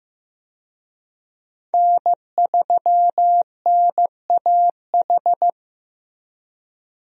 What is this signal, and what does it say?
Morse code, heard as a keyed tone, about 16 wpm, 700 Hz. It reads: N3NAH
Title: USA Call Signs - 5 Characters - Repeated 2x - 15wpm